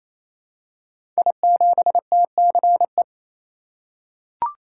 Morse code, heard as a keyed tone, about 29 wpm, 700 Hz. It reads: I7TCE
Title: Code Groups - 5 Letter:Numbers - 28wpm